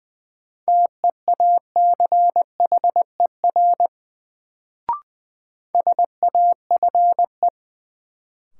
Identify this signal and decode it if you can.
Morse code, heard as a keyed tone, about 20 wpm, 700 Hz. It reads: TEACHER SAFE
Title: Sentences from Top 900 Words - New Word Drill - 20wpm